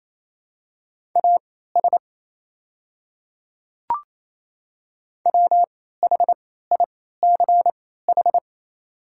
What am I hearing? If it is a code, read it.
Morse code, heard as a keyed tone, about 28 wpm, spaced as for 16 wpm, 700 Hz. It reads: AS WHICH
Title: 3x Character Spacing - Top 100 Words - 28wpm